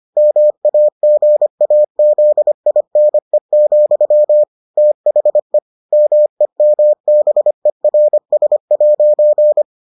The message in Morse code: MAGAZINE, THE MEMBERS'

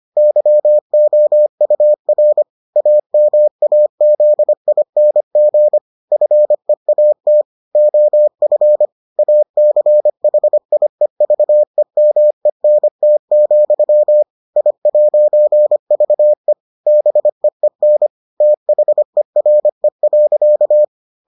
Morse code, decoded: YOUR AMAZING FEAT OF ACHIEVEMENT, I'VE BEEN THERE.

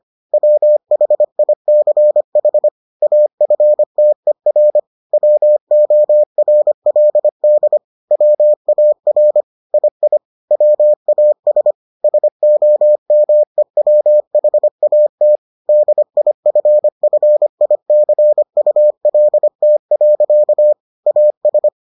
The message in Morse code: WHICH AFTER WORLD WAR II WAS SOMEWHAT DIFFICULT. AS

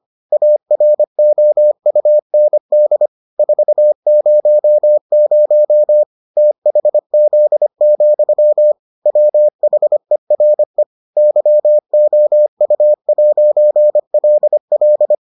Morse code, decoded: AROUND 400 THZ, WHERE YOU'LL

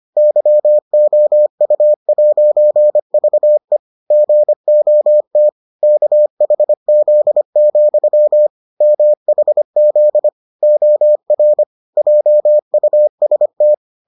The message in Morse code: YOU'VE GOT KHZ, MHZ OR JUST

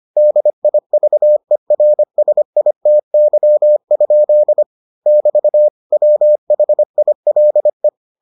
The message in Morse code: DIVERSITY? <BT> WHILE

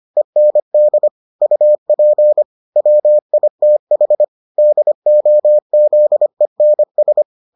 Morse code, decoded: END UP WITH DOZENS

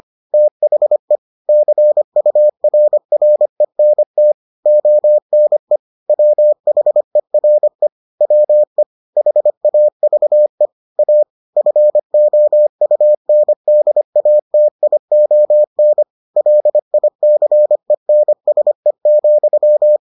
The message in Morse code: THE CURRENT ONE WHERE WE HAVE A FOUNDATION LICENSE,

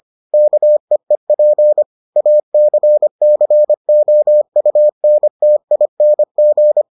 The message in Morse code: KEEP ACCOUNTING

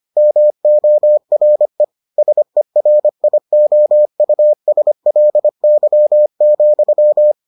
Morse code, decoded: MORE SERIOUSLY,

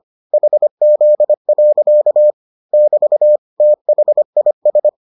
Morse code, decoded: HZ. <BT> THIS

